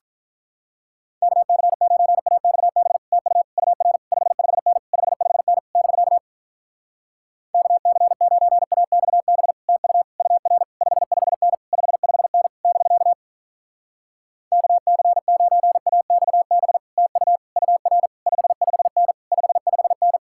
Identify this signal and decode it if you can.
Morse code, heard as a keyed tone, about 47 wpm, 700 Hz. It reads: KC9AXB TU UR 55N 55N <BK> KC9AXB TU UR 55N 55N <BK> KC9AXB TU UR 55N 55N